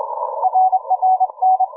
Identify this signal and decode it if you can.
Morse code, heard as a keyed tone, about 25 wpm, 750 Hz. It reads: RRN